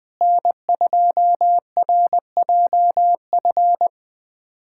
Morse code, heard as a keyed tone, about 21 wpm, 700 Hz. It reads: N2RJF
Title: USA Callsign DE Callsign - 5 Characters - 20wpm